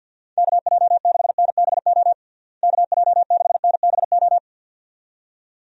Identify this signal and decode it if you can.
Morse code, heard as a keyed tone, about 47 wpm, 700 Hz. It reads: KJ6NBO KJ6NBO
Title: USA Call Signs - 6 Characters - Repeated 2x - 50wpm